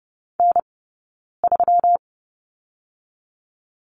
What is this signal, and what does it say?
Morse code, heard as a keyed tone, about 30 wpm, spaced as for 9 wpm, 700 Hz. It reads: N3
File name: Lesson 20d - Number 3 - Two Characters 30wpm